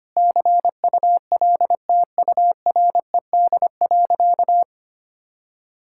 Morse code, heard as a keyed tone, about 25 wpm, 700 Hz. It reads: CULTURED.
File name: practice / 25wpm / All-Age Appropriate Jokes - 25wpm